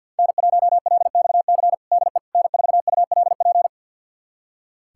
Morse code, heard as a keyed tone, about 48 wpm, 700 Hz. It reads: N1LXC DE N4ULP
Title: USA Callsign DE Callsign - 5 Characters - 50wpm